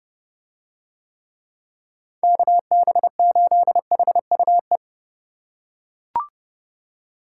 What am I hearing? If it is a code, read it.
Morse code, heard as a keyed tone, about 29 wpm, 700 Hz. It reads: KB8HUE